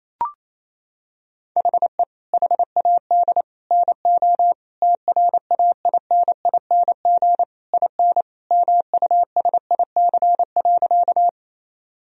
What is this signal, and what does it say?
Morse code, heard as a keyed tone, about 28 wpm, 700 Hz. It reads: HE HAD NO TRAINING IN MUSIC.
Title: Sentences from Top 1000 Words - Review Part 4 - 28wpm